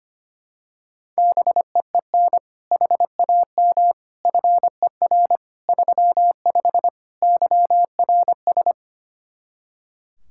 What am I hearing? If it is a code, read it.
Morse code, heard as a keyed tone, about 25 wpm, 700 Hz. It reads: BEEN HAM FER 35 YRS
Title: Practice QSO - Exchange 3 - More Info - 25wpm